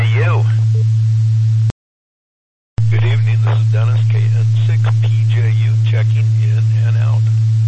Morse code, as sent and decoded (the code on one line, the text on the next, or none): none